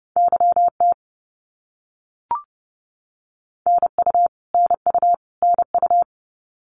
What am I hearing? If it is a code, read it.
Morse code, heard as a keyed tone, about 30 wpm, 700 Hz. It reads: YT NU NU NU
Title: Canadian Provinces and Territories - Repeated 3x - 30wpm